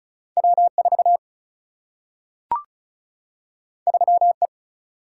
Morse code, heard as a keyed tone, about 36 wpm, 700 Hz. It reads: W4 3E
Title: Single-Letter Single-Number 35wpm